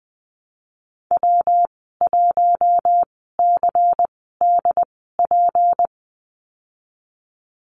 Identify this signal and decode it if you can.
Morse code, heard as a keyed tone, about 20 wpm, spaced as for 15 wpm, 700 Hz. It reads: W1CDP
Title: 154 Lesson 25g - Letter p - Call Signs 20wpm